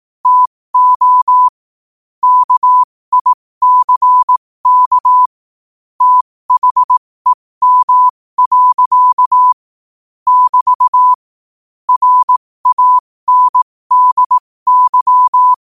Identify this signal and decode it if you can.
Morse code, heard as a keyed tone, about 18 wpm, spaced as for 15 wpm, 1000 Hz. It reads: TO KICK THEM. <BT> RANDY
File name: qotdfiles / 15wpm